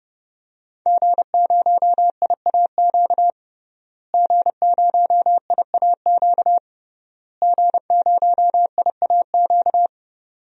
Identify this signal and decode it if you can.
Morse code, heard as a keyed tone, about 30 wpm, 700 Hz. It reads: G0IAQ G0IAQ G0IAQ
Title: DXCC-100 Call Signs - 5 Characters - Repeated 3x - 30wpm